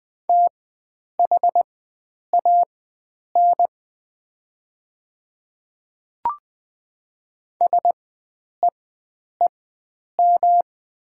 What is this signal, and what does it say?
Morse code, heard as a keyed tone, about 20 wpm, spaced as for 9 wpm, 700 Hz. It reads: THAN SEEM
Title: 4x Character Spacing - Top 500 Words - 20wpm